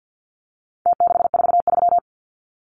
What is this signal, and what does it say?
Morse code, heard as a keyed tone, about 49 wpm, 700 Hz. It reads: T643
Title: Numbers - Contest Style - 4-Digit 50wpm